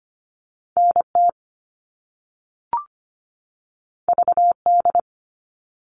Morse code, heard as a keyed tone, about 25 wpm, 700 Hz. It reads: N T V D
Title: All Cut Numbers - 2-Digit 25wpm